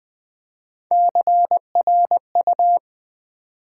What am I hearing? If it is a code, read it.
Morse code, heard as a keyed tone, about 20 wpm, 700 Hz. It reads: CRU
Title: Build A Word - 1 Letter At A Time - 3 to 5-Letter Words - 20wpm